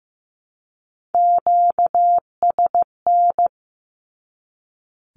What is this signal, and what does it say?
Morse code, heard as a keyed tone, about 15 wpm, 700 Hz. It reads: QSN
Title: Q-Codes 15wpm